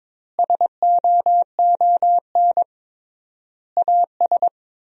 Morse code, heard as a keyed tone, about 21 wpm, 700 Hz. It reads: SOON AS
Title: 3x Word-Spacing - Sentences from Top 500 Words - 22wpm